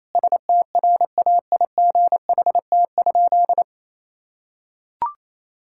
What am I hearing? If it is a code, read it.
Morse code, heard as a keyed tone, about 28 wpm, 700 Hz. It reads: STRAIGHT?
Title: Sentences from Top 1000 Words - Review Part 6 - 28wpm